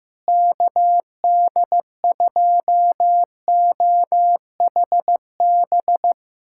Morse code, heard as a keyed tone, about 15 wpm, 700 Hz. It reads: KD2OHB
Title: USA Callsign DE Callsign - 6 Characters - 15wpm